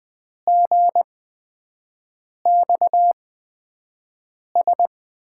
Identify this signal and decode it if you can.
Morse code, heard as a keyed tone, about 20 wpm, spaced as for 5 wpm, 700 Hz. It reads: GXS
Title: USA Call Signs - 6 Characters - 8x Character Spacing - 20wpm